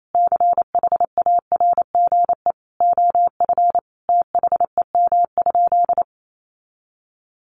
Morse code, heard as a keyed tone, about 28 wpm, 700 Hz. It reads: CHARGE OF THEM?
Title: Sentences from Top 900 Words 28wpm